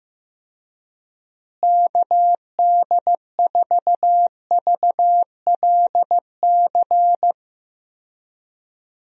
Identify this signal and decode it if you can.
Morse code, heard as a keyed tone, about 15 wpm, 700 Hz. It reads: KD4VLC
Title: USA Call Signs - 6 Characters - Repeated 2x - 15wpm